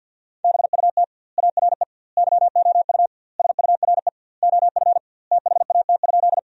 Morse code, heard as a keyed tone, about 46 wpm, 700 Hz. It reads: BUT ARE YOU SURE OF THAT?
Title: 2x Word-Spacing - Sentences from Top 300 Words - 50wpm